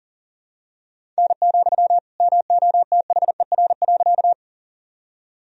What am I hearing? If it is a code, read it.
Morse code, heard as a keyed tone, about 37 wpm, 700 Hz. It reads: N, MOTHER.
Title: Sentences from Top 200 Words and Abbreviations 40wpm